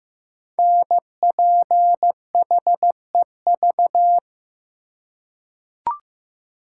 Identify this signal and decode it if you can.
Morse code, heard as a keyed tone, about 15 wpm, 700 Hz. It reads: NPHEV
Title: Code Groups - 5 Letters - 15wpm